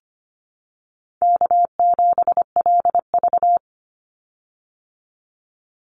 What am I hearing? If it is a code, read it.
Morse code, heard as a keyed tone, about 25 wpm, 700 Hz. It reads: K7LV
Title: Call Signs Repeated 2x - 25wpm